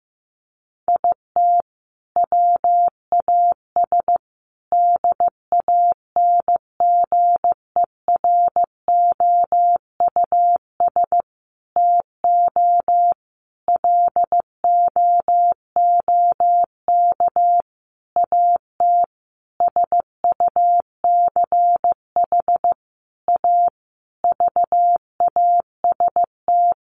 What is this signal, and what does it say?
Morse code, heard as a keyed tone, about 15 wpm, 700 Hz. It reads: IT WAS DANGEROUS TO LOOK AT SUCH A VAST
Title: Sentences from Top 1300 Words 15wpm